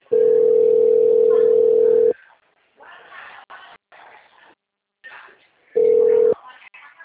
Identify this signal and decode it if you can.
Ringback tone (North America), 440 + 480 Hz, with speech.
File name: out- -2013- - - .38715